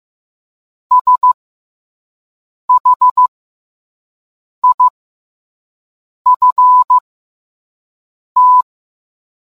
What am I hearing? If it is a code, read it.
Morse code, heard as a keyed tone, about 15 wpm, spaced as for 5 wpm, 1000 Hz. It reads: SHIFT